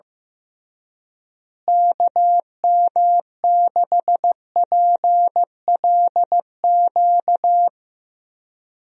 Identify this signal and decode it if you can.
Morse code, heard as a keyed tone, about 15 wpm, 700 Hz. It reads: KM6PLQ
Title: USA Call Signs - 6 Characters - Repeated 3x - 15wpm